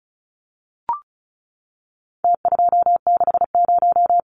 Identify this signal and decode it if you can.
Morse code, heard as a keyed tone, about 34 wpm, 700 Hz. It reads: T260
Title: Numbers - Contest Style - 4-Digit 35wpm